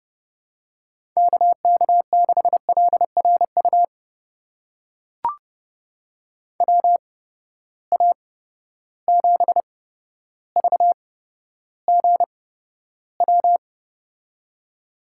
Morse code, heard as a keyed tone, about 32 wpm, spaced as for 13 wpm, 700 Hz. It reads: KK6LRU WA7VGW
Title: USA Call Signs - 6 Characters - 8x Character Spacing - 30wpm